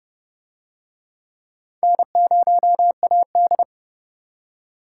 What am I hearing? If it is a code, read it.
Morse code, heard as a keyed tone, about 29 wpm, 700 Hz. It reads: N0AD